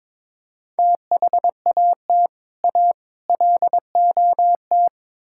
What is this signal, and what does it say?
Morse code, heard as a keyed tone, about 22 wpm, 700 Hz. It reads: THAT A LOT
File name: practice / 22wpm / Build A Phrase - 5-Word Phrases - 22wpm